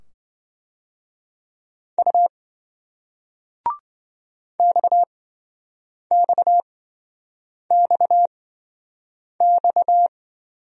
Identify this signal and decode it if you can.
Morse code, heard as a keyed tone, about 28 wpm, spaced as for 7 wpm, 700 Hz. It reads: U XXXX